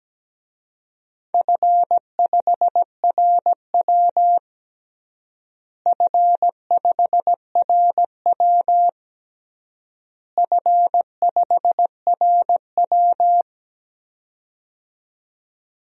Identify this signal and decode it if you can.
Morse code, heard as a keyed tone, about 17 wpm, 700 Hz. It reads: F5RW F5RW F5RW
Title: DXCC-100 Call Signs - 4 Characters - Repeated 3x - 17wpm